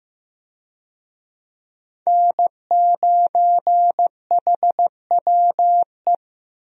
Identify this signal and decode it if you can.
Morse code, heard as a keyed tone, about 15 wpm, 700 Hz. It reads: N9HWE
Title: USA Call Signs - 5 Characters - Repeated 2x - 15wpm